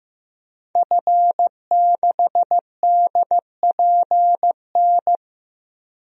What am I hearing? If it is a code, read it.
Morse code, heard as a keyed tone, about 15 wpm, 700 Hz. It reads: F6DPN
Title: DXCC-100 Call Signs - 5 Characters - 15wpm